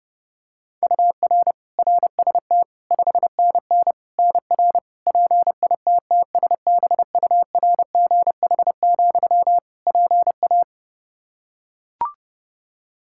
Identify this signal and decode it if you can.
Morse code, heard as a keyed tone, about 32 wpm, 700 Hz. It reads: UR RST 5NN NR PITTSBURGH, PA
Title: Practice QSO - Exchange 1 - Beginning - 30wpm